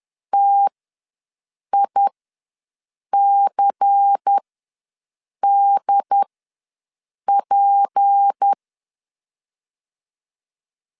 Morse code, heard as a keyed tone, about 10 wpm, spaced as for 6 wpm, 800 Hz. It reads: TICDP
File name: LEZIONI / 6 PKX0 / 6 P ws 10wpm 3751